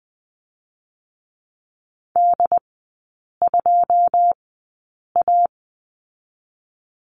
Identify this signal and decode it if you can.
Morse code, heard as a keyed tone, about 20 wpm, spaced as for 8 wpm, 700 Hz. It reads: D2A
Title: 100 Lesson 17e - Letter u - Three Characters 20wpm